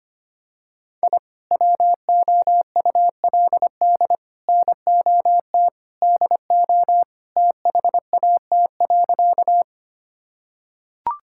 Morse code, heard as a keyed tone, about 25 wpm, 700 Hz. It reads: I WOULD NOT DO THAT.
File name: practice / 25wpm / Sentences with 3-Word Phrases - 25wpm